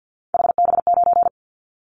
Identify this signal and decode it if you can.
Morse code, heard as a keyed tone, about 50 wpm, 700 Hz. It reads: HB9
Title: Callsign Prefixes for Top 25 DXCC - 50wpm